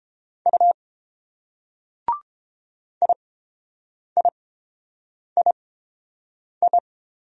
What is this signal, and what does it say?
Morse code, heard as a keyed tone, about 28 wpm, spaced as for 7 wpm, 700 Hz. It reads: U IIII